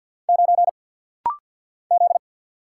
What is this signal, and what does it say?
Morse code, heard as a keyed tone, about 50 wpm, 700 Hz. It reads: 9 Z
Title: Single Letter-Number Alt - Rapid-Fire 50wpm